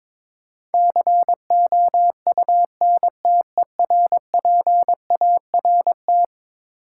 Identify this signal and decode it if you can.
Morse code, heard as a keyed tone, about 22 wpm, 700 Hz. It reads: COUNTERPART
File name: practice / 22wpm / Eleven Letter Words - 22wpm